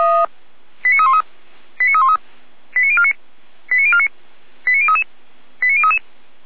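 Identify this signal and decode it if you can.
Five-tone selective call (ZVEI), heard as quick runs of five tones, tones of about 80 ms. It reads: 89313 89313 89049 89049 89033 89033
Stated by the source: ZVEI five-tone selective call received by SDR.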